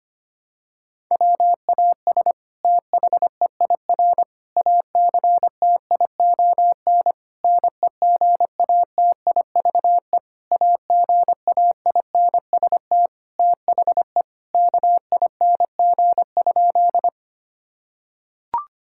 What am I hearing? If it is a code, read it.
Morse code, heard as a keyed tone, about 25 wpm, 700 Hz. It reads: WAS THEIR ACTION NEGATIVE AGAINST THE KING?
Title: Sentences from Top 1000 Words - Review Part 6 - 25wpm